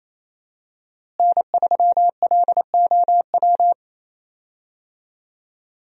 Morse code, heard as a keyed tone, about 28 wpm, 700 Hz. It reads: N3LOW
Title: USA Call Signs - 5 Characters - Repeated 3x - 28wpm